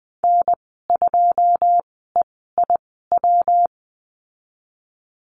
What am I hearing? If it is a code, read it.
Morse code, heard as a keyed tone, about 20 wpm, spaced as for 15 wpm, 700 Hz. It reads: N2EIW
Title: 147 Lesson 24g - Letter y - Call Signs 20wpm